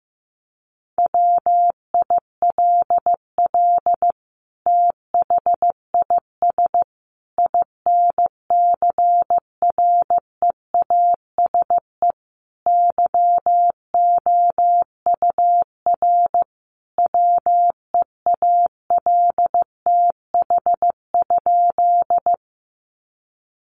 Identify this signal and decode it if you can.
Morse code, heard as a keyed tone, about 15 wpm, 700 Hz. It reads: WILL THIS INCREASE YOUR WEALTH?
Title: Sentences from Top 1200 Words 15wpm